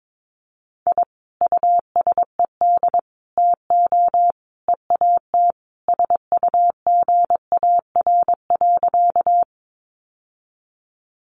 Morse code, heard as a keyed tone, about 22 wpm, 700 Hz. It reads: I USED TO EAT SUGAR.
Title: Sentences from Top 1200 Words 22wpm